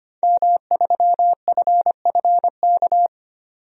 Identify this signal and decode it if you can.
Morse code, heard as a keyed tone, about 25 wpm, 700 Hz. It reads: M3FFK
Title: DXCC-100 Call Signs - 5 Characters - 25wpm